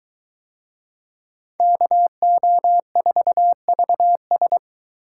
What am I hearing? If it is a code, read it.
Morse code, heard as a keyed tone, about 23 wpm, 700 Hz. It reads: KO4VS